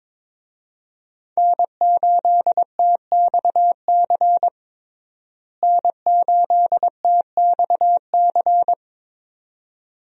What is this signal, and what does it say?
Morse code, heard as a keyed tone, about 21 wpm, 700 Hz. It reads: N8TXC N8TXC
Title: USA Call Signs - 5 Characters - Repeated 2x - 22wpm